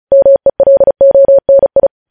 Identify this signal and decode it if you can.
Morse code, heard as a keyed tone, about 36 wpm, 550 Hz. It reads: MELONI
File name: podcast / News headlines-1x-35x35